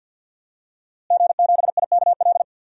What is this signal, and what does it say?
Morse code, heard as a keyed tone, about 52 wpm, 700 Hz. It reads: G7IKL
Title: DXCC-100 Call Signs - 5 Characters - Repeated 2x - 50wpm